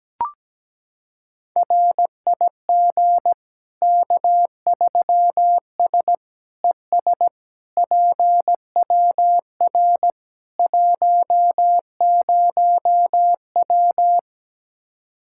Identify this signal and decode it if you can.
Morse code, heard as a keyed tone, about 17 wpm, 700 Hz. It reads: RIG K3S ES PWR 10W
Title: Practice QSO - Exchange 2 - Info - 17wpm